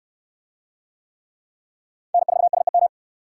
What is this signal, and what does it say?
Morse code, heard as a keyed tone, about 58 wpm, 700 Hz. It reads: N4SEM